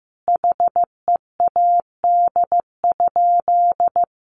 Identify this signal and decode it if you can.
Morse code, heard as a keyed tone, about 15 wpm, 700 Hz. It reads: HEAD?
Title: Sentences from Top 300 Words 15wpm